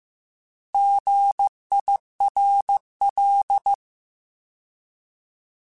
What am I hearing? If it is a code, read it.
Morse code, heard as a keyed tone, about 15 wpm, 800 Hz. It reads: GIRL